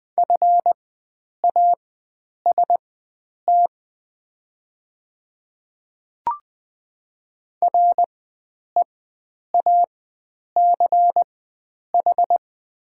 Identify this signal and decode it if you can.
Morse code, heard as a keyed tone, about 20 wpm, spaced as for 9 wpm, 700 Hz. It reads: FAST REACH
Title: 4x Character Spacing - Top 500 Words - 20wpm